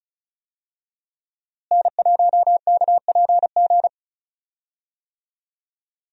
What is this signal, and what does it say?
Morse code, heard as a keyed tone, about 34 wpm, 700 Hz. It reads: N1KPG
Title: USA Call Signs - 5 Characters - Repeated 2x - 35wpm